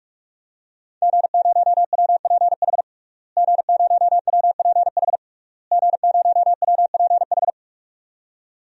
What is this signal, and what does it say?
Morse code, heard as a keyed tone, about 46 wpm, 700 Hz. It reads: G0WPH G0WPH G0WPH
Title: DXCC-100 Call Signs - 5 Characters - Repeated 3x - 45wpm